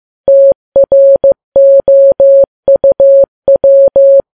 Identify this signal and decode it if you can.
Morse code, heard as a keyed tone, about 15 wpm, 550 Hz. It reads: TROUW
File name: Files Web tls / Morse on9fv / morse15kps